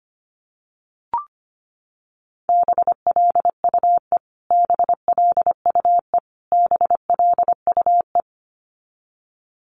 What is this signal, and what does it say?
Morse code, heard as a keyed tone, about 25 wpm, 700 Hz. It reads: BLUE BLUE BLUE
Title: Top 500 Words Repeated 25wpm